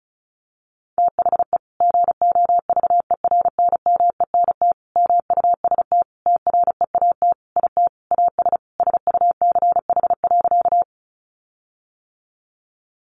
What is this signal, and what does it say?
Morse code, heard as a keyed tone, about 34 wpm, 700 Hz. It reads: THE GOVERNMENT MUST TREAT IT AS SUCH.
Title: Sentences from Top 1300 Words 35wpm